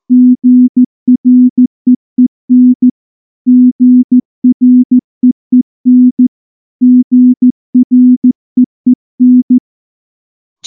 Morse code, heard as a keyed tone, about 14 wpm, 250 Hz. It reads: GREEN GREEN GREEN
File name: morse / morse-training-15WPM